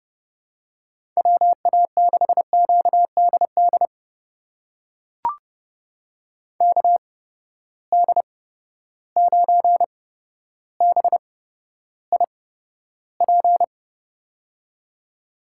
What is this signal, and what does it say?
Morse code, heard as a keyed tone, about 29 wpm, spaced as for 13 wpm, 700 Hz. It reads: WA6QDD KD9BIP
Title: USA Call Signs - 6 Characters - 8x Character Spacing - 30wpm